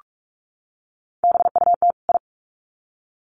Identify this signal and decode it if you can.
Morse code, heard as a keyed tone, about 44 wpm, 700 Hz. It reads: BUT I
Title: Sets of 2 Words 45wpm